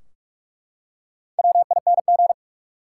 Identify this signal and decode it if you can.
Morse code, heard as a keyed tone, about 48 wpm, 700 Hz. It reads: WING
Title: Four Letter Words - 45wpm